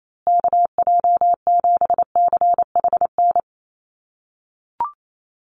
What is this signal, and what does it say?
Morse code, heard as a keyed tone, about 28 wpm, 700 Hz. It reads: KJ7CHN
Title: USA Call Signs Repeated 3x 28wpm